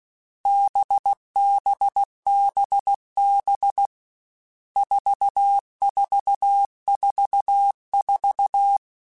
Morse code, heard as a keyed tone, about 16 wpm, 800 Hz. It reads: BBBB 4444
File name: Groups@13 / Groups@13 mp3s / VB46@13